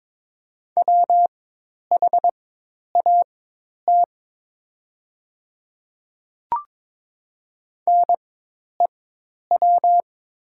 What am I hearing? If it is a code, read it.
Morse code, heard as a keyed tone, about 22 wpm, spaced as for 10 wpm, 690 Hz. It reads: WHAT NEW
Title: 4x Character Spacing - Top 200 Words - 22wpm